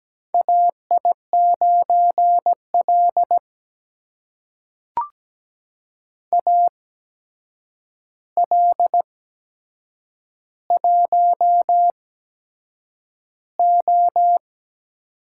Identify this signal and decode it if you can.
Morse code, heard as a keyed tone, about 17 wpm, spaced as for 7 wpm, 700 Hz. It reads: AI9L AL1O
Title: USA Call Signs - 4 Characters - 8x Character Spacing - 17wpm